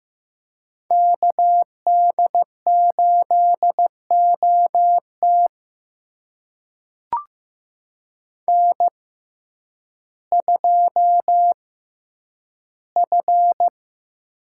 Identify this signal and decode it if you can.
Morse code, heard as a keyed tone, about 15 wpm, spaced as for 9 wpm, 700 Hz. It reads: KD8OT N2F
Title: USA Call Signs - 5 Characters - 6x Character Spacing - 15wpm